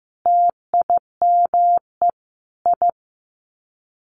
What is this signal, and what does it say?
Morse code, heard as a keyed tone, about 15 wpm, 700 Hz. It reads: TIME I
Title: Sets of 4 Words 15wpm